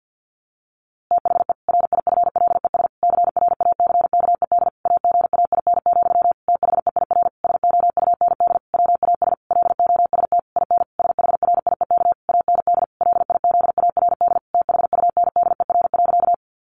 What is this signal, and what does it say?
Morse code, heard as a keyed tone, about 49 wpm, 700 Hz. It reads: THE RIFLES CRACKED AGAIN, THEIR SOUND WAS LOST IN SHRIEK AND LIZARD THUNDER.
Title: A Sound Of Thunder 50wpm